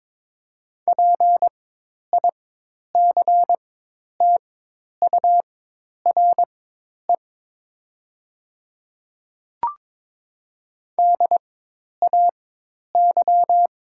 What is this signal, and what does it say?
Morse code, heard as a keyed tone, about 21 wpm, spaced as for 10 wpm, 700 Hz. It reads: PICTURE DAY
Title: 4x Character Spacing - Top 200 Words - 22wpm